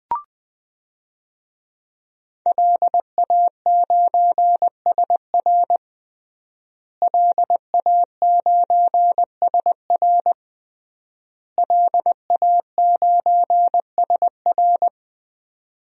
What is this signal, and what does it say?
Morse code, heard as a keyed tone, about 20 wpm, 700 Hz. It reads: LA9SR LA9SR LA9SR
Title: DXCC-100 Call Signs - 5 Characters - Repeated 3x - 20wpm